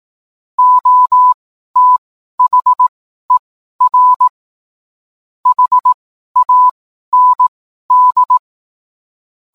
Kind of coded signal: Morse code